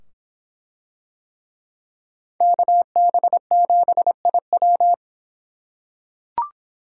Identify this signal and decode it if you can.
Morse code, heard as a keyed tone, about 25 wpm, 700 Hz. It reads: KB7IW